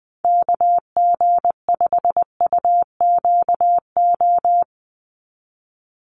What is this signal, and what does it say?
Morse code, heard as a keyed tone, about 20 wpm, 700 Hz. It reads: KG5UQO
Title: USA Call Signs 20wpm